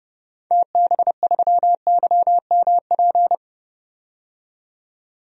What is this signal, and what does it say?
Morse code, heard as a keyed tone, about 30 wpm, 690 Hz. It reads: TB3YMP
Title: DXCC-100 Call Signs - 6 Characters - 30wpm